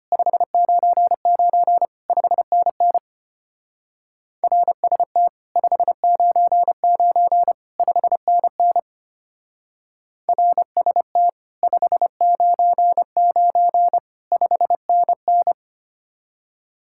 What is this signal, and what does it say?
Morse code, heard as a keyed tone, about 32 wpm, 700 Hz. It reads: 599 5NN RST 599 5NN RST 599 5NN